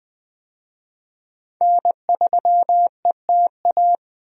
Morse code, heard as a keyed tone, about 20 wpm, 700 Hz. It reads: N3ETA
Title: USA Call Signs - 5 Characters - Repeated 2x - 20wpm